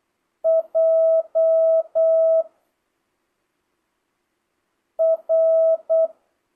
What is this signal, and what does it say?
Morse code, heard as a keyed tone, about 8 wpm, 650 Hz. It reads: J R
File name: jh8cbh / 2025cw